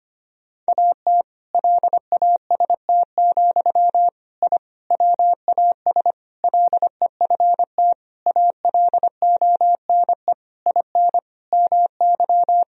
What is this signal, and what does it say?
Morse code, heard as a keyed tone, about 25 wpm, 700 Hz. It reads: AT LAST, I WAS LEFT ALONE IN MY
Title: Sentences from Top 1000 Words - Review Part 1 - 25wpm